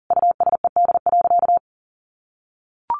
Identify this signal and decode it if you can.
Morse code, heard as a keyed tone, about 40 wpm, 700 Hz. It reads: USED.
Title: Sentences from Top 700 Words 40wpm